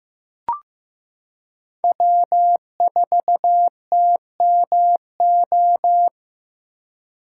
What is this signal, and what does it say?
Morse code, heard as a keyed tone, about 16 wpm, 700 Hz. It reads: W4TMO
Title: K1USN SST QSO Practice - 15wpm